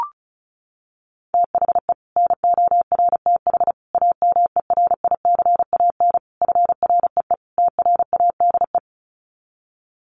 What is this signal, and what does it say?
Morse code, heard as a keyed tone, about 34 wpm, 700 Hz. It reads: THE NORTH AMERICAN FREE TRADE
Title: Sets of 5 Words Encore 35wpm